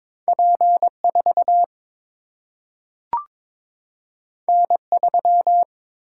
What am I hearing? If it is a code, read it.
Morse code, heard as a keyed tone, about 22 wpm, 700 Hz. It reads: P4 N3
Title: Single-Letter Single-Number 22wpm